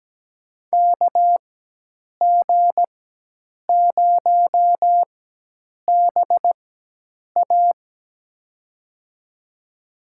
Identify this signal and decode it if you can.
Morse code, heard as a keyed tone, about 17 wpm, spaced as for 8 wpm, 700 Hz. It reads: KG0BA
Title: USA Call Signs - 5 Characters - 4x Character Spacing - 17wpm